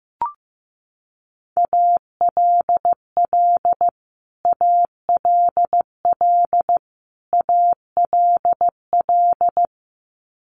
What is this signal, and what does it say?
Morse code, heard as a keyed tone, about 15 wpm, 700 Hz. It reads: ALL ALL ALL
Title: Top 200 Words Repeated 15wpm